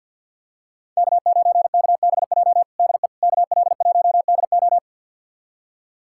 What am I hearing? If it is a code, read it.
Morse code, heard as a keyed tone, about 51 wpm, 700 Hz. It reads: K9KBJ DE KL1DO
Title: USA Callsign DE Callsign - 5 Characters - 50wpm